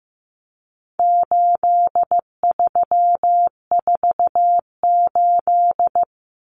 Morse code, heard as a keyed tone, about 15 wpm, 700 Hz. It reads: 8348
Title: Common Cut Numbers - 4-Digit 15wpm